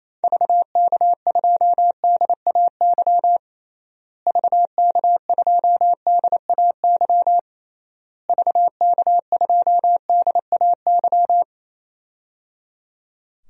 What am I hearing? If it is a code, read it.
Morse code, heard as a keyed tone, about 28 wpm, 700 Hz. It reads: VK2DAY VK2DAY VK2DAY
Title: DXCC-100 Call Signs - 6 Characters - Repeated 3x - 28wpm